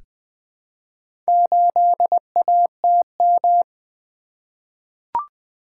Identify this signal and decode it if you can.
Morse code, heard as a keyed tone, about 20 wpm, 700 Hz. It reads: 8ATM
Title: Four Letter-Number Combinations - 20wpm